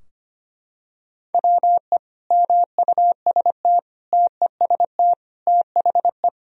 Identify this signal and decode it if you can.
Morse code, heard as a keyed tone, about 25 wpm, 700 Hz. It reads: WE MUST TEST THE
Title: Sentences from Top 1000 Words - Review Part 3 - 25wpm